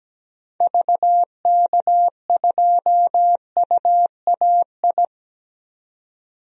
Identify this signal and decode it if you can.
Morse code, heard as a keyed tone, about 17 wpm, 700 Hz. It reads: VK2UAI
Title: DXCC-100 Call Signs - 6 Characters - 17wpm